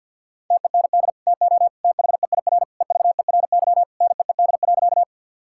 Morse code, heard as a keyed tone, about 50 wpm, 700 Hz. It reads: TEND TO THEIR EVERY NEED.
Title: Sentences from Top 1000 Words - Review Part 5 - 50wpm